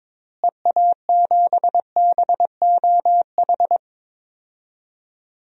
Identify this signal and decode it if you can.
Morse code, heard as a keyed tone, about 22 wpm, 700 Hz. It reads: EA7BOH
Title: DXCC-100 Call Signs - 6 Characters - Repeated 3x - 22wpm